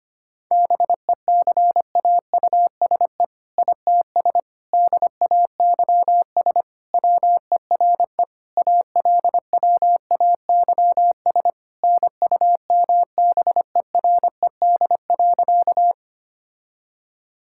Morse code, heard as a keyed tone, about 25 wpm, 700 Hz. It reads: BECAUSE ITS DAYS WERE ALWAYS NUMBERED.
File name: practice / 25wpm / All-Age Appropriate Jokes - 25wpm